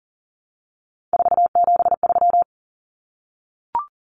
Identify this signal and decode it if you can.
Morse code, heard as a keyed tone, about 40 wpm, 700 Hz. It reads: 473
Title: Numbers - 3-Digit 40wpm